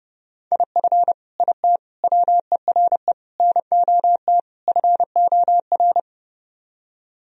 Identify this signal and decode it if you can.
Morse code, heard as a keyed tone, about 30 wpm, 700 Hz. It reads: IF IT WERE NOT FOR